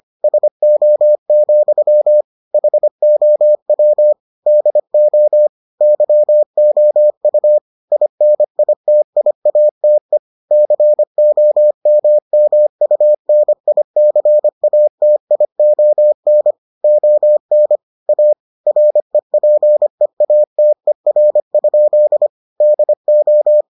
Morse code, heard as a keyed tone, about 25 wpm, 600 Hz. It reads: SO, HOW DO YOU INITIATE COMMUNICATION ON A REPEATER? DO